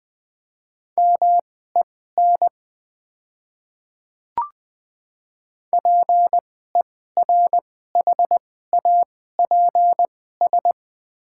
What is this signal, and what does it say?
Morse code, heard as a keyed tone, about 20 wpm, spaced as for 14 wpm, 700 Hz. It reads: MEN PERHAPS
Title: 2x Character Spacing - Top 500 Words - 20wpm